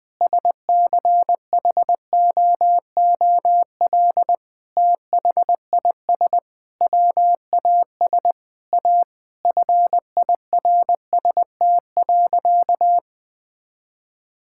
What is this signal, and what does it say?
Morse code, heard as a keyed tone, about 20 wpm, 700 Hz. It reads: SCHOOL THIS WAS A FIRST.
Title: Sentences with 5-Word Phrases - 20wpm